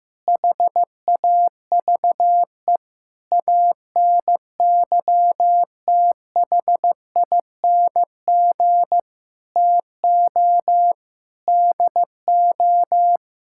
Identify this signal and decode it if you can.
Morse code, heard as a keyed tone, about 15 wpm, 700 Hz. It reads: HAVE ANYTHING TO DO
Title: Sentences from Top 1000 Words - Review Part 5 - 15wpm